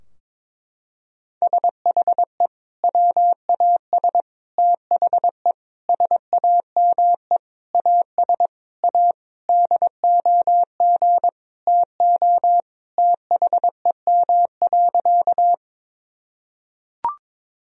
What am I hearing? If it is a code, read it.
Morse code, heard as a keyed tone, about 21 wpm, 700 Hz. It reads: SHE WAS THE SAME AS A DOG TO THEM.
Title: Sentences from Top 1000 Words - Review Part 2 - 22wpm